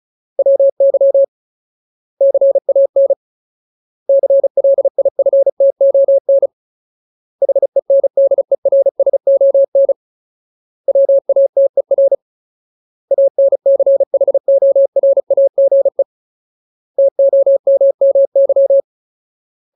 Morse code, heard as a keyed tone, about 34 wpm, 550 Hz. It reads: WY CAN CLIFTON HENDERSON WATER ANCHORAGE TOMMY
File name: Ultimate Word List / 35WPM Ultimate